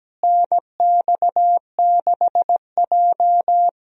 Morse code, heard as a keyed tone, about 17 wpm, 700 Hz. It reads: NX6J
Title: USA Call Signs - 4 Characters - Repeated 3x - 17wpm